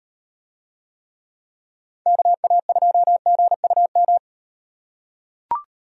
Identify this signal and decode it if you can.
Morse code, heard as a keyed tone, about 38 wpm, 700 Hz. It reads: KA2GUM